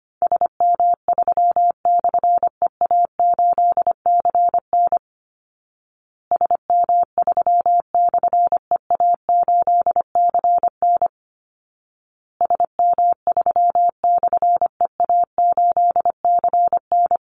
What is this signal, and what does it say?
Morse code, heard as a keyed tone, about 25 wpm, 700 Hz. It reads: SM3/EA8CN SM3/EA8CN SM3/EA8CN
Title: Call Signs Repeated 2x - 25wpm